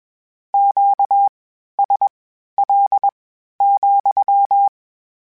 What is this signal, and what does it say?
Morse code, heard as a keyed tone, about 21 wpm, spaced as for 12 wpm, 800 Hz. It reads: QSL,
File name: Q Calls / Q Calls-mp3s / Q Calls sppc @20